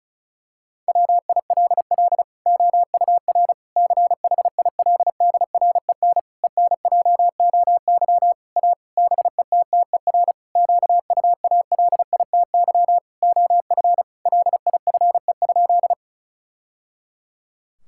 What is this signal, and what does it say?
Morse code, heard as a keyed tone, about 34 wpm, 700 Hz. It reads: WILL OUR CHILDREN ENJOY A BETTER QUALITY OF LIFE?
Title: Sentences from Top 1000 Words - Review Part 5 - 35wpm